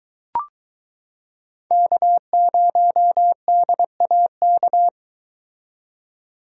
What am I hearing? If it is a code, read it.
Morse code, heard as a keyed tone, about 22 wpm, 700 Hz. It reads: K0DAK